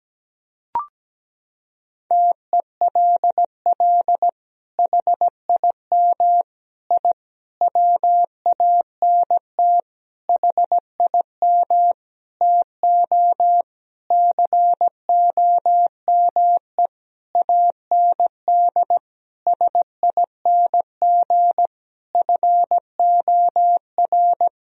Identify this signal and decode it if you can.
Morse code, heard as a keyed tone, about 17 wpm, 700 Hz. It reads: TELL HIM I WANT HIM TO COME AND SING FOR
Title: Sentences from Top 1000 Words - Review Part 2 - 17wpm